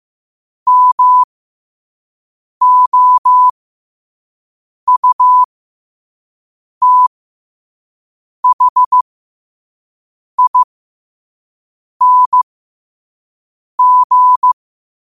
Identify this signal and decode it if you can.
Morse code, heard as a keyed tone, about 15 wpm, spaced as for 5 wpm, 1000 Hz. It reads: MOUTHING